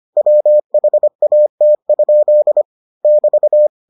Morse code, heard as a keyed tone, about 25 wpm, 600 Hz. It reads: WHAT? <BT>